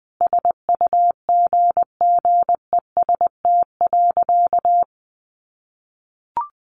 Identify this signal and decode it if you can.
Morse code, heard as a keyed tone, about 20 wpm, 700 Hz. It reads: SUGGEST.
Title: Sentences from Top 1500 Words 20wpm